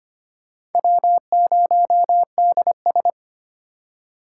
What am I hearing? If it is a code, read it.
Morse code, heard as a keyed tone, about 25 wpm, 700 Hz. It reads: W0DS
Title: USA Call Signs - 4 Characters - 4x Character Spacing - 25wpm